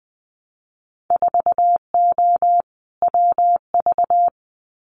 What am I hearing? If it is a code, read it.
Morse code, heard as a keyed tone, about 20 wpm, 700 Hz. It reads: 4O WV
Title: Winter Field Day Exchanges Encore 20wpm